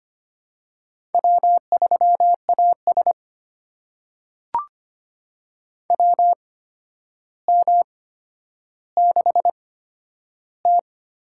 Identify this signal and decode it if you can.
Morse code, heard as a keyed tone, about 25 wpm, spaced as for 11 wpm, 700 Hz. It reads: W3AS WM6T
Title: USA Call Signs - 4 Characters - 8x Character Spacing - 25wpm